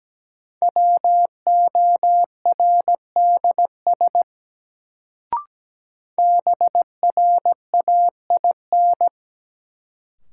Morse code, heard as a keyed tone, about 17 wpm, 700 Hz. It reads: WORDS BRAIN
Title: Sentences from Top 1000 Words - Review Part 6 - New Word Drill - 17wpm